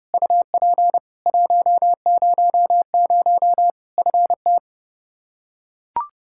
Morse code, heard as a keyed tone, about 30 wpm, 700 Hz. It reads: UP 100 FT
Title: Practice QSO - Exchange 2 - Info - 30wpm